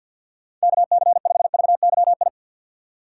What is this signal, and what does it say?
Morse code, heard as a keyed tone, about 53 wpm, 700 Hz. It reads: KC5VYI
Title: USA Call Signs - 6 Characters - 6x Character Spacing - 50wpm